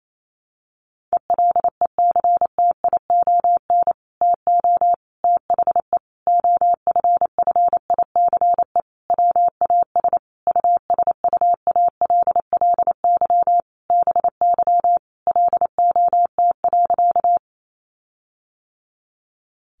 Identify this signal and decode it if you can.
Morse code, heard as a keyed tone, about 28 wpm, 700 Hz. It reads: ELECTION TO THE OFFICE WAS USUALLY BY LOT.
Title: Sentences from Top 1500 Words 28wpm